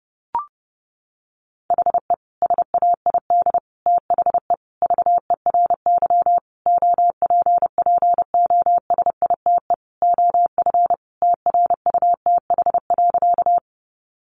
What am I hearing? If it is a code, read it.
Morse code, heard as a keyed tone, about 30 wpm, 700 Hz. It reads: HE SAID THE VERY OPPOSITE OF TRUTH.
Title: Sentences from Top 1300 Words 30wpm